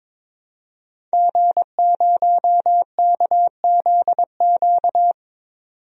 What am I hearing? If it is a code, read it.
Morse code, heard as a keyed tone, about 22 wpm, 700 Hz. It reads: G0KZQ
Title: DXCC-100 Call Signs - 5 Characters - 22wpm